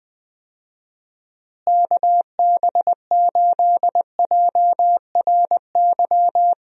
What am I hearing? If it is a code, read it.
Morse code, heard as a keyed tone, about 21 wpm, 700 Hz. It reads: KB8JRY
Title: USA Call Signs - 6 Characters - Repeated 2x - 20wpm